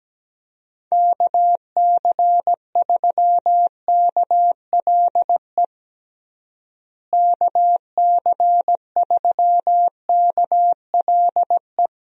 Morse code, heard as a keyed tone, about 17 wpm, 700 Hz. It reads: KC3KLE KC3KLE
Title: USA Call Signs - 6 Characters - Repeated 3x - 17wpm